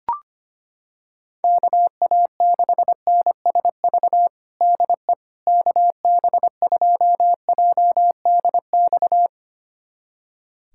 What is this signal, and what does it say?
Morse code, heard as a keyed tone, about 25 wpm, 700 Hz. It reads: KA6NSV DE KB2JDX
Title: USA Callsign DE Callsign 25wpm